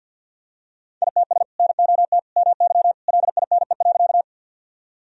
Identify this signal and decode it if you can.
Morse code, heard as a keyed tone, about 50 wpm, 700 Hz. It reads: ITS NOT MY LINE.
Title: Sentences from Top 1000 Words - Review Part 1 - 50wpm